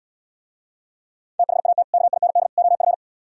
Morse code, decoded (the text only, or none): THAT KIND OF